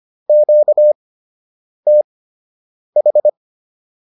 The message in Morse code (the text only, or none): QTH